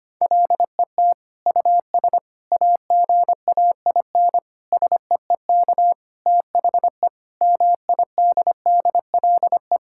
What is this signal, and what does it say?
Morse code, heard as a keyed tone, about 25 wpm, 700 Hz. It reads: LET US AGAIN SEEK THE MIDDLE
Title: Sentences from Top 1600 Words 25wpm